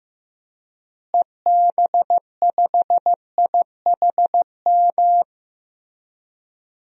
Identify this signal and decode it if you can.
Morse code, heard as a keyed tone, about 15 wpm, 700 Hz. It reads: EB5IHM
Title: DXCC-100 Call Signs - 6 Characters - Repeated 3x - 15wpm